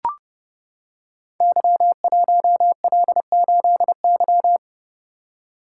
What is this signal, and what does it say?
Morse code, heard as a keyed tone, about 28 wpm, 700 Hz. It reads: Y1L8Y
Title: Code Groups - 5 Letter:Numbers - 30wpm